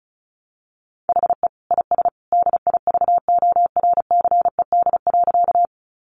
Morse code, heard as a keyed tone, about 34 wpm, 700 Hz. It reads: HE IS DIVORCED.
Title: Additional Easy Phrases 35wpm